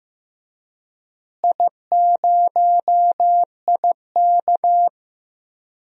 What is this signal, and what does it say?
Morse code, heard as a keyed tone, about 15 wpm, 700 Hz. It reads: I0IK
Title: DXCC-100 Call Signs - 4 Characters - Repeated 3x - 15wpm